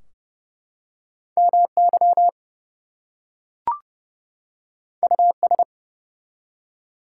Morse code, heard as a keyed tone, about 29 wpm, 700 Hz. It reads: MY US
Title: Two Letter Words - 30wpm